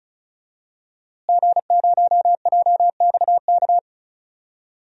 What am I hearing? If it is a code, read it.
Morse code, heard as a keyed tone, about 36 wpm, 700 Hz. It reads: G0JXK
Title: DXCC-100 Call Signs - 5 Characters - Repeated 2x - 35wpm